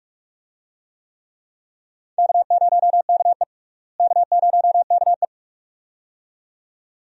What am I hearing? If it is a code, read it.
Morse code, heard as a keyed tone, about 46 wpm, 700 Hz. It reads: K0KE K0KE
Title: USA Call Signs - 4 Characters - Repeated 2x - 45wpm